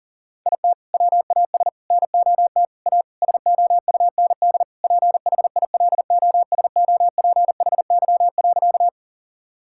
Morse code, heard as a keyed tone, about 40 wpm, 700 Hz. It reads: IT WAS NOT A SOUND PHILOSOPHY.
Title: Sentences from Top 1000 Words - Review Part 6 - 40wpm